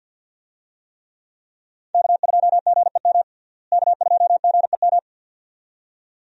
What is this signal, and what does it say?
Morse code, heard as a keyed tone, about 50 wpm, 700 Hz. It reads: K2GEM K2GEM
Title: USA Call Signs - 5 Characters - Repeated 2x - 50wpm